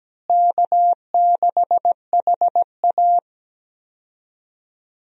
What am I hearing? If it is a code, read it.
Morse code, heard as a keyed tone, about 17 wpm, 700 Hz. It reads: K6HA
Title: USA Call Signs - 4 Characters - Repeated 3x - 17wpm